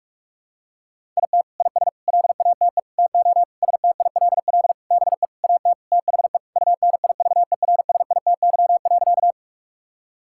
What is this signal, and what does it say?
Morse code, heard as a keyed tone, about 46 wpm, 700 Hz. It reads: IT IS LATE TO STILL BE AT THE UNIVERSITY.
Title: Sentences from Top 1000 Words - Review Part 5 - 45wpm